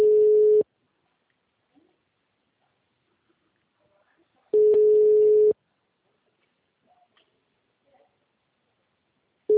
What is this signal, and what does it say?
Ringback tone (Europe, 425 Hz).